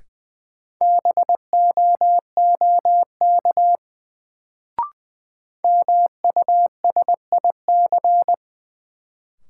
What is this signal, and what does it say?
Morse code, heard as a keyed tone, about 20 wpm, 700 Hz. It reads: BOOK MUSIC
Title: Sentences from Top 300 Words - New Word Drill - 20wpm